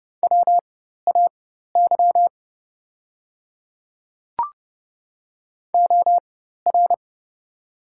Morse code, heard as a keyed tone, about 30 wpm, spaced as for 14 wpm, 700 Hz. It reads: WAY OR
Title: 4x Character Spacing - Top 100 Words - 30wpm